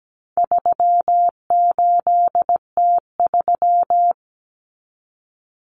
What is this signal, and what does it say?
Morse code, heard as a keyed tone, about 17 wpm, 700 Hz. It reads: 38T3
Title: Common Cut Numbers - 4-Digit 17wpm